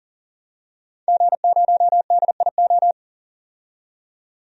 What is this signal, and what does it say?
Morse code, heard as a keyed tone, about 40 wpm, 700 Hz. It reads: G0DIO
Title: DXCC-100 Call Signs - 5 Characters - 40wpm